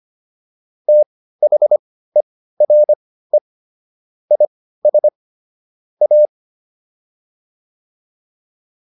Morse code, heard as a keyed tone, about 25 wpm, spaced as for 15 wpm, 600 Hz. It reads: THERE IS A
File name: WD204 15